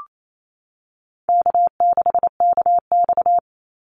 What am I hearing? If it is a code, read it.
Morse code, heard as a keyed tone, about 28 wpm, 700 Hz. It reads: K6KX
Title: Easy Call Signs Repeated 2x - 28wpm